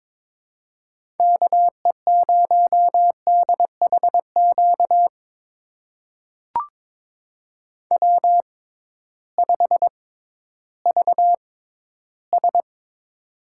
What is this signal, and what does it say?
Morse code, heard as a keyed tone, about 22 wpm, spaced as for 13 wpm, 700 Hz. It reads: KE0DHQ W5VS
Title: USA Call Signs - 6x Character Spacing - 22wpm